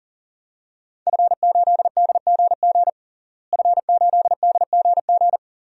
Morse code, heard as a keyed tone, about 40 wpm, 700 Hz. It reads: F8DGG F8DGG
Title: DXCC-100 Call Signs - 5 Characters - Repeated 3x - 40wpm